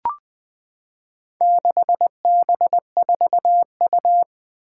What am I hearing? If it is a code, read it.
Morse code, heard as a keyed tone, about 21 wpm, 700 Hz. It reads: 6B4U
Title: Four Challenging Letter-Number Combinations 20wpm